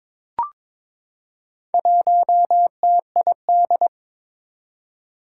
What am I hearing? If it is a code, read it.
Morse code, heard as a keyed tone, about 21 wpm, 700 Hz. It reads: 1TID
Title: Four Challenging Letter-Number Combinations 22wpm